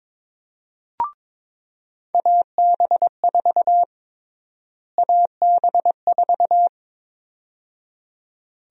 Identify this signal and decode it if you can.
Morse code, heard as a keyed tone, about 22 wpm, 700 Hz. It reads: AB4 AB4
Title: USA Call Sign Prefixes - Repeated 2x - 22wpm